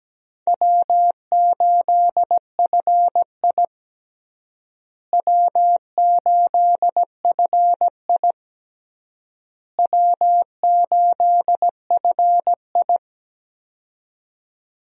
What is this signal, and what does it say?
Morse code, heard as a keyed tone, about 17 wpm, 700 Hz. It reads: W8FI W8FI W8FI
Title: USA Call Signs - 4 Characters - Repeated 3x - 17wpm